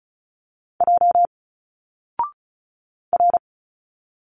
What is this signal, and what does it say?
Morse code, heard as a keyed tone, about 36 wpm, 700 Hz. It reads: J R
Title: Single Letters 35wpm